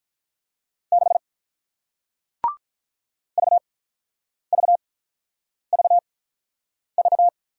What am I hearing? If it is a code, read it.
Morse code, heard as a keyed tone, about 45 wpm, spaced as for 9 wpm, 700 Hz. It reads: 6 VVVV